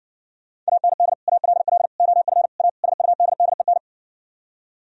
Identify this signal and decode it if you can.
Morse code, heard as a keyed tone, about 60 wpm, 700 Hz. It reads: AND ALL OF A SUDDEN